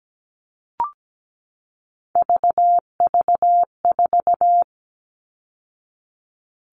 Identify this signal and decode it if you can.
Morse code, heard as a keyed tone, about 17 wpm, 700 Hz. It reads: VV4
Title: All Cut Numbers - 3-Digit 17wpm